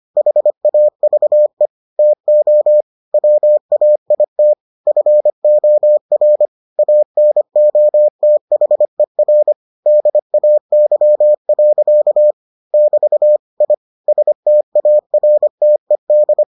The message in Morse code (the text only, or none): HAVE TO WAIT FOR ANOTHER DAY. <BT> I STARTED